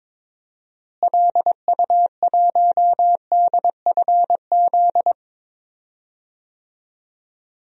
Morse code, heard as a keyed tone, about 22 wpm, 700 Hz. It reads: LU1DFZ
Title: DXCC-100 Call Signs - 6 Characters - Repeated 2x - 22wpm